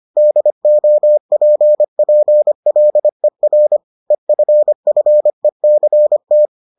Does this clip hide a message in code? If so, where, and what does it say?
Yes: Morse code from 0.2 to 6.5 s, heard as a keyed tone, about 25 wpm, 600 Hz. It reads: DOPPLER EFFECT